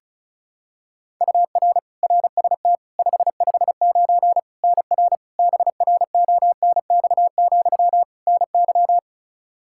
Morse code, heard as a keyed tone, about 36 wpm, 700 Hz. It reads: UR RST 559 NR BRONX, NY
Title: Practice QSO - Exchange 1 - Beginning - 35wpm